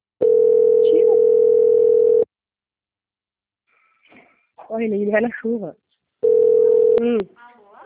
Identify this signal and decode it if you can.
Ringback tone (North America), 440 + 480 Hz, with speech.
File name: out- - - - - .9601